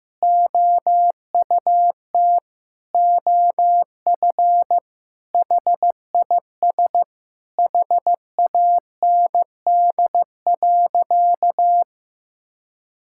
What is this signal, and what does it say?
Morse code, heard as a keyed tone, about 16 wpm, 700 Hz. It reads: OUT OF HIS HAND.
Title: Sentences from Top 1600 Words 15wpm